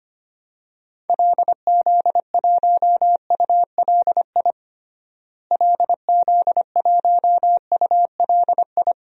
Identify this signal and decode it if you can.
Morse code, heard as a keyed tone, about 25 wpm, 700 Hz. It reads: LZ1ULI LZ1ULI
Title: DXCC-100 Call Signs - 6 Characters - Repeated 2x - 25wpm